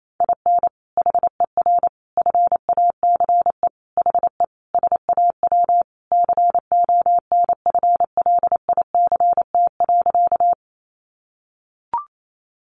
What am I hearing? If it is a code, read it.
Morse code, heard as a keyed tone, about 28 wpm, 700 Hz. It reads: IN HER FACE HE SAW CONFLICT.
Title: Sentences from Top 1300 Words 28wpm